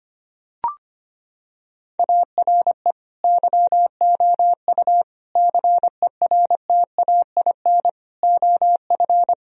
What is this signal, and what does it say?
Morse code, heard as a keyed tone, about 25 wpm, 700 Hz. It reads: ARE YOU CERTAIN OF
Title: Sentences from Top 1000 Words - Review Part 2 - 25wpm